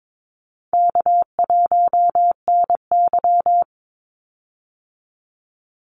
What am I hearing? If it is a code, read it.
Morse code, heard as a keyed tone, about 22 wpm, 700 Hz. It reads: K1NY
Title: Easy Call Signs - 22wpm